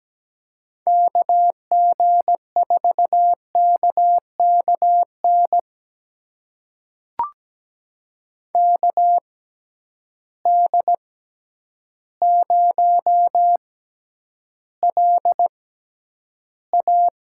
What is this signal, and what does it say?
Morse code, heard as a keyed tone, about 17 wpm, spaced as for 9 wpm, 700 Hz. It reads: KG4KKN KD0LA
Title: USA Call Signs - 6 Characters - 6x Character Spacing - 17wpm